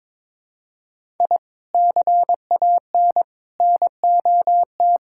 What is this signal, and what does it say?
Morse code, heard as a keyed tone, about 22 wpm, 700 Hz. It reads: I CAN NOT
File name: practice / 22wpm / Sentences with 3-Word Phrases - 22wpm